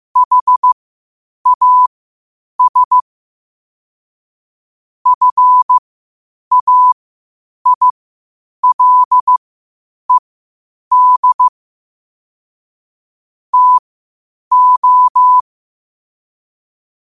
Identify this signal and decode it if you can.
Morse code, heard as a keyed tone, about 15 wpm, spaced as for 8 wpm, 1000 Hz. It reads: HAS FAILED TO
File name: qotdfiles / 8wpm